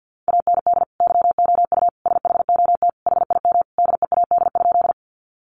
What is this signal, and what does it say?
Morse code, heard as a keyed tone, about 50 wpm, 700 Hz. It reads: AND YOU SHOT HIM DEAD?
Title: Sentences from Top 1500 Words 50wpm